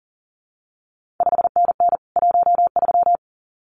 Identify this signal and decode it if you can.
Morse code, heard as a keyed tone, about 39 wpm, 700 Hz. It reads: 5NN 13
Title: CQ WW DX Contest Exchange Encore - 40wpm